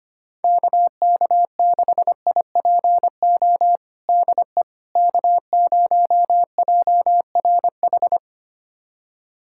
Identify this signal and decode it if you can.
Morse code, heard as a keyed tone, about 25 wpm, 700 Hz. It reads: KK6IPO DE K0JRH
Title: USA Callsign DE Callsign 25wpm